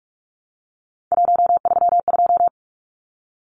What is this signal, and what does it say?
Morse code, heard as a keyed tone, about 44 wpm, 700 Hz. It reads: 132
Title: Numbers - 3-Digit 45wpm